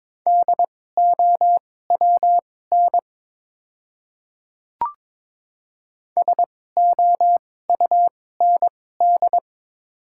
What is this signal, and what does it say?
Morse code, heard as a keyed tone, about 22 wpm, spaced as for 16 wpm, 690 Hz. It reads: DOWN SOUND
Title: 2x Character Spacing - Top 100 Words - 22wpm